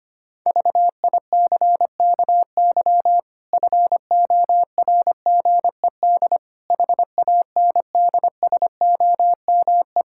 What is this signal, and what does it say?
Morse code, heard as a keyed tone, about 25 wpm, 700 Hz. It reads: VICKY FORGED HANDSOME
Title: Pangram Challenge v2 - 25wpm